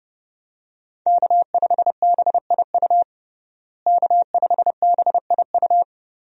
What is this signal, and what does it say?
Morse code, heard as a keyed tone, about 30 wpm, 700 Hz. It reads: K5BIU K5BIU
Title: USA Call Signs - 5 Characters - Repeated 2x - 30wpm